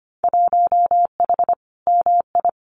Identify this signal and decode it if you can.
Morse code, heard as a keyed tone, about 25 wpm, 700 Hz. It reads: 1H MI
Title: Winter Field Day Exchanges 25wpm